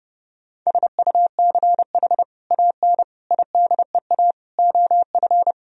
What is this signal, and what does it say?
Morse code, heard as a keyed tone, about 30 wpm, 700 Hz. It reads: SUCH AN IDEA OF
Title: Sentences from Top 1000 Words - Review Part 3 - 30wpm